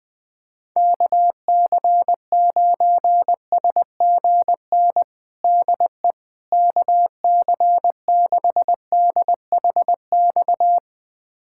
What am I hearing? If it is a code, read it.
Morse code, heard as a keyed tone, about 21 wpm, 700 Hz. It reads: KC9SGN DE KC6DHX
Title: USA Callsign DE Callsign 20wpm